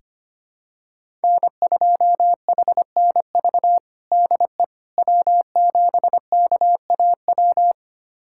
Morse code, heard as a keyed tone, about 25 wpm, 700 Hz. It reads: N2HNV DE W7KAW
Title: USA Callsign DE Callsign - 5 Characters - 25wpm